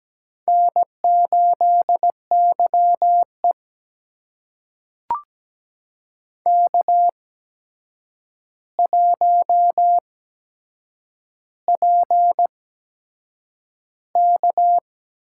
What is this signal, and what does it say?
Morse code, heard as a keyed tone, about 17 wpm, spaced as for 7 wpm, 700 Hz. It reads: N8YE K1PK
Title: USA Call Signs - 4 Characters - 8x Character Spacing - 17wpm